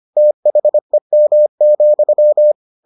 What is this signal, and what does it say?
Morse code, heard as a keyed tone, about 25 wpm, 600 Hz. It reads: THEM,